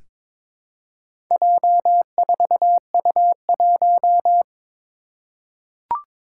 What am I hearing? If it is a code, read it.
Morse code, heard as a keyed tone, about 22 wpm, 700 Hz. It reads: J4U1
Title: Four Challenging Letter-Number Combinations 22wpm